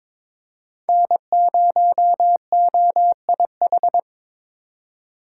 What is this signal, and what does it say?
Morse code, heard as a keyed tone, about 22 wpm, 700 Hz. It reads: N0OIH
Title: USA Call Signs - 5 Characters - 4x Character Spacing - 22wpm